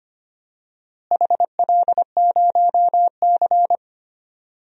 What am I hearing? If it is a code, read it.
Morse code, heard as a keyed tone, about 25 wpm, 700 Hz. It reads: HL0C
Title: DXCC-100 Call Signs - 4 Characters - Repeated 3x - 25wpm